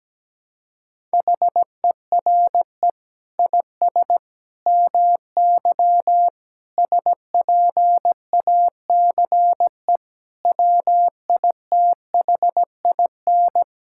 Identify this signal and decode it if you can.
Morse code, heard as a keyed tone, about 17 wpm, 700 Hz. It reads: HERE IS MY SPACE WITHIN
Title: Sentences from Top 1000 Words - Review Part 5 - 17wpm